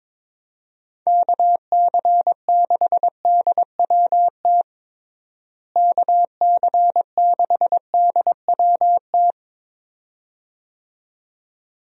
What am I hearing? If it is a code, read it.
Morse code, heard as a keyed tone, about 22 wpm, 700 Hz. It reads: KC6DWT KC6DWT
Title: USA Call Signs - 6 Characters - Repeated 2x - 22wpm